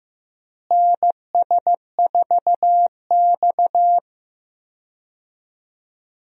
Morse code, heard as a keyed tone, about 15 wpm, 700 Hz. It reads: NS4X
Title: USA Call Signs - 4 Characters - Repeated 3x - 15wpm